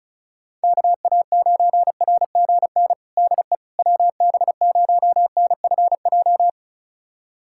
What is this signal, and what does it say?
Morse code, heard as a keyed tone, about 34 wpm, 690 Hz. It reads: KA9RGN DE WB0NFJ
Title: USA Callsign DE Callsign - 6 Characters - 35wpm